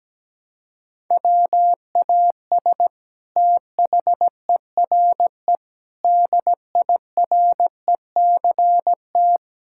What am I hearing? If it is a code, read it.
Morse code, heard as a keyed tone, about 17 wpm, 700 Hz. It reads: WAS THERE DIRECT
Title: Sentences from Top 1000 Words - Review Part 6 - 17wpm